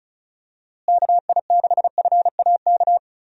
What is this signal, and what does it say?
Morse code, heard as a keyed tone, about 36 wpm, 700 Hz. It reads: KI6FAK
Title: USA Call Signs - 6 Characters - Repeated 3x - 35wpm